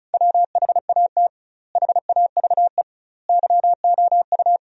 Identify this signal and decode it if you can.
Morse code, heard as a keyed tone, about 36 wpm, 700 Hz. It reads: WHAT HAVE YOU
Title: 2x Word-Spacing - Sentences from Top 500 Words - 35wpm